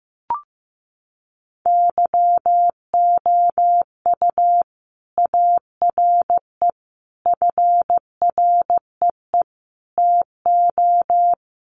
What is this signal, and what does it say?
Morse code, heard as a keyed tone, about 15 wpm, 700 Hz. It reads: YOU ARE FREE TO
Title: Sentences from Top 500 Words 15wpm